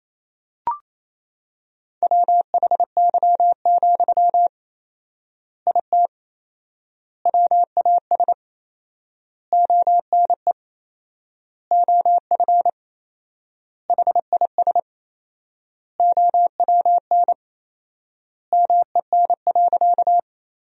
Morse code, heard as a keyed tone, about 28 wpm, 700 Hz. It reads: WHY, IT WAS ONE OF HIS OWN MEN.
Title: 4x Word-Spacing - Sentences from Top 300 Words - 28wpm